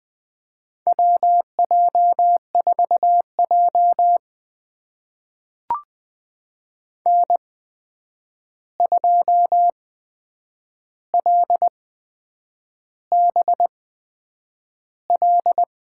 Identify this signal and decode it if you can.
Morse code, heard as a keyed tone, about 20 wpm, spaced as for 8 wpm, 700 Hz. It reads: WJ4J N2LBL
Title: USA Call Signs - 8x Character Spacing - 20wpm